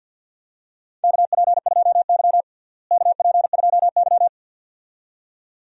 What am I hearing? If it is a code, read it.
Morse code, heard as a keyed tone, about 52 wpm, 700 Hz. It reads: KP2Y KP2Y
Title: DXCC-100 Call Signs - 4 Characters - Repeated 2x - 50wpm